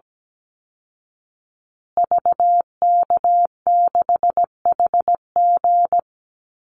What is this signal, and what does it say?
Morse code, heard as a keyed tone, about 17 wpm, 700 Hz. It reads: VK6HG
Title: Call Signs Repeated 3x - 17wpm